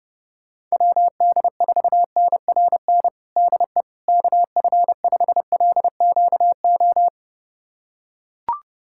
Morse code, heard as a keyed tone, about 30 wpm, 700 Hz. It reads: WD4NRN DE KF5LQO
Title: USA Callsign DE Callsign - 6 Characters - 30wpm